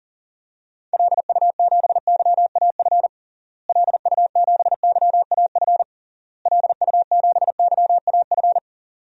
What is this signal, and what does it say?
Morse code, heard as a keyed tone, about 37 wpm, 700 Hz. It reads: LU7YAF LU7YAF LU7YAF
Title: DXCC-100 Call Signs - 6 Characters - Repeated 3x - 40wpm